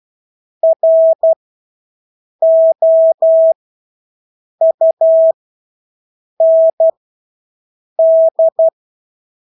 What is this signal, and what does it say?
Morse code, heard as a keyed tone, about 12 wpm, spaced as for 6 wpm, 650 Hz. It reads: ROUND
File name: Engelske ord 12-6 wpm